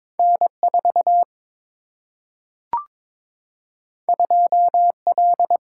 Morse code, heard as a keyed tone, about 22 wpm, 700 Hz. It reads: N4 2L
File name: practice / 22wpm / Two Letter-Number Combinations - 22wpm